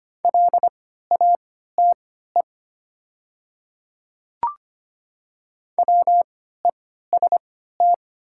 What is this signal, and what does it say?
Morse code, heard as a keyed tone, about 25 wpm, spaced as for 14 wpm, 700 Hz. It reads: LATE WEST
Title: 3x Character Spacing - Top 500 Words - 25wpm